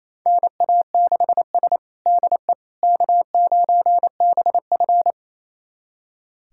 Morse code, heard as a keyed tone, about 28 wpm, 700 Hz. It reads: NA6S DE K9BF
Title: USA Callsign DE Callsign - 4 Characters - 28wpm